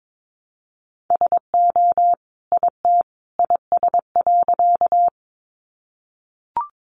Morse code, heard as a keyed tone, about 22 wpm, 700 Hz. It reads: SO IT IS.
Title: Sentences from Top 100 Words 22wpm